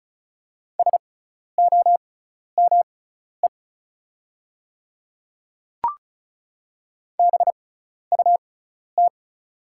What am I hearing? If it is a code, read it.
Morse code, heard as a keyed tone, about 36 wpm, spaced as for 12 wpm, 690 Hz. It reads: SOME BUT
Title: 6x Character Spacing - Top 100 Words - 35wpm